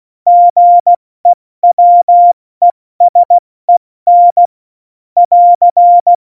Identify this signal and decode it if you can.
Morse code, heard as a keyed tone, about 16 wpm, spaced as for 14 wpm, 700 Hz. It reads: GEWESEN <AR>
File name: Morsekurs / Witze / Witz 19 Mai, 2020 70 F80